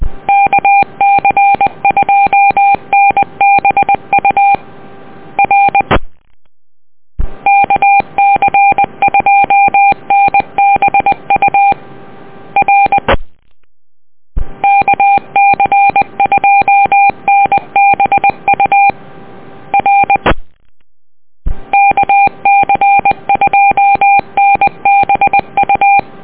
Morse code, heard as a keyed tone, about 20 wpm, 800 Hz. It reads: KC2NBU R KC2NBU R KC2NBU R KC2NBU